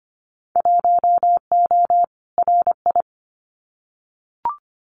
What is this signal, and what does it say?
Morse code, heard as a keyed tone, about 25 wpm, 700 Hz. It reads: 1O RI
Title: Winter Field Day Exchanges 25wpm